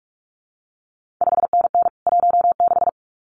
Morse code, heard as a keyed tone, about 46 wpm, 700 Hz. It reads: 5NN 16
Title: CQ WW DX Contest Exchange - 45wpm